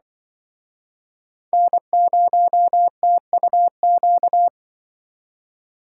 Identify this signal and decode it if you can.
Morse code, heard as a keyed tone, about 25 wpm, 700 Hz. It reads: N0TUQ
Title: Speed Racing - USA 1x3 Call Signs 20wpm